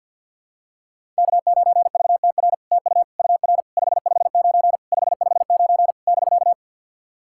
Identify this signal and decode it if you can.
Morse code, heard as a keyed tone, about 51 wpm, 690 Hz. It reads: K9VTR TU UR 559 559 <BK>
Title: SOTA QSO Practice 50wpm